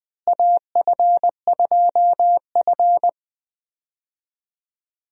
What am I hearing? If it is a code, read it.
Morse code, heard as a keyed tone, about 20 wpm, 700 Hz. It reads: AF2F
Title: USA Call Signs - 4 Characters - Repeated 3x - 20wpm